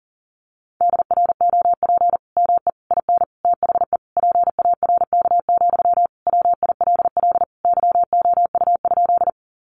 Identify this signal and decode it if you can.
Morse code, heard as a keyed tone, about 37 wpm, 700 Hz. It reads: DROP ME IN THE PARK, WILL YOU?
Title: Sentences from Top 1400 Words 40wpm